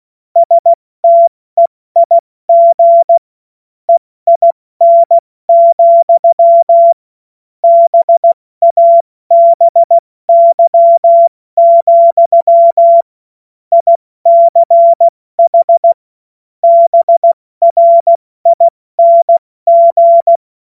Morse code, heard as a keyed tone, about 16 wpm, 680 Hz. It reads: STEIG EIN, BABY, ICH BRING
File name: Morsekurs / Witze / Witz 24. August 2021, 70 F80